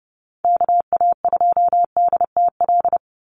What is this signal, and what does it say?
Morse code, heard as a keyed tone, about 30 wpm, 700 Hz. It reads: KA2DTL
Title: USA Call Signs 30wpm